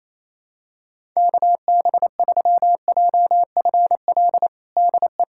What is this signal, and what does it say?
Morse code, heard as a keyed tone, about 27 wpm, 700 Hz. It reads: KB3JFL DE
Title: USA Callsign DE Callsign 28wpm